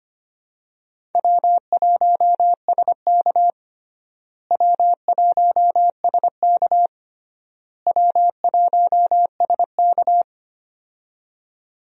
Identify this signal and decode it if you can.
Morse code, heard as a keyed tone, about 25 wpm, 700 Hz. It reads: W1SK W1SK W1SK
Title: USA Call Signs - 4 Characters - Repeated 3x - 25wpm